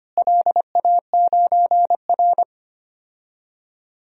Morse code, heard as a keyed tone, about 25 wpm, 700 Hz. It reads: LA9R
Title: DXCC-100 Call Signs - 4 Characters - Repeated 2x - 25wpm